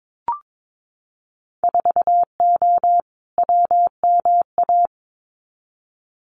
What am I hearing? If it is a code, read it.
Morse code, heard as a keyed tone, about 22 wpm, 700 Hz. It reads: 4O WMA
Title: Winter Field Day Exchanges Encore 22wpm